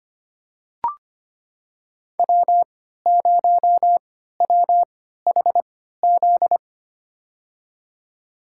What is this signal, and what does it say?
Morse code, heard as a keyed tone, about 25 wpm, spaced as for 14 wpm, 700 Hz. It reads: W0WHZ
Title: USA Call Signs - 3x Character Spacing - 25wpm